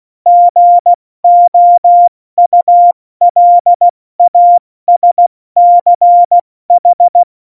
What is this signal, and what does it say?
Morse code, heard as a keyed tone, about 16 wpm, 700 Hz. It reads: GOULASCH